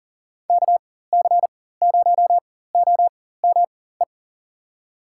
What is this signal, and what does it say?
Morse code, heard as a keyed tone, about 40 wpm, spaced as for 19 wpm, 700 Hz. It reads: KC0OME
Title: USA Call Signs - 6 Characters - 4x Character Spacing - 40wpm